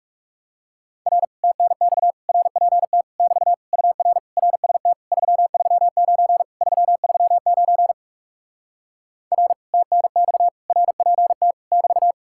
Morse code, heard as a keyed tone, about 40 wpm, 700 Hz. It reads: R TNX RPT <BT> UR RST 339 339 R TNX RPT <BT>